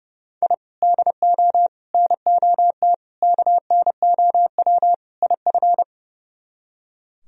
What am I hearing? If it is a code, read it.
Morse code, heard as a keyed tone, about 40 wpm, 700 Hz. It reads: I DO NOT KNOW IF